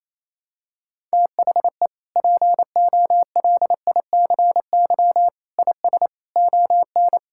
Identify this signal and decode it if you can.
Morse code, heard as a keyed tone, about 28 wpm, 700 Hz. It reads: THE POLICY IS ON
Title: Sentences from Top 1000 Words - Review Part 5 - 28wpm